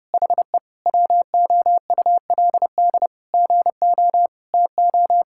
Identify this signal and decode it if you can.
Morse code, heard as a keyed tone, about 30 wpm, 700 Hz. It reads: HE WOULD GO TO
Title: Sentences from Top 1600 Words 30wpm